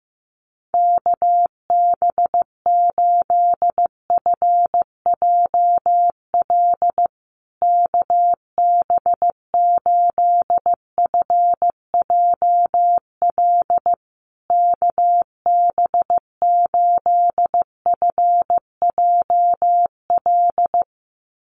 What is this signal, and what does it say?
Morse code, heard as a keyed tone, about 15 wpm, 700 Hz. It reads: KB8FJL KB8FJL KB8FJL
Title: USA Call Signs Repeated 3x 15wpm